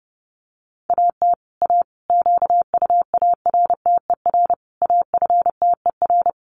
Morse code, heard as a keyed tone, about 30 wpm, 700 Hz. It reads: AT A QUARTER AFTER
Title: Sentences from Top 1300 Words 30wpm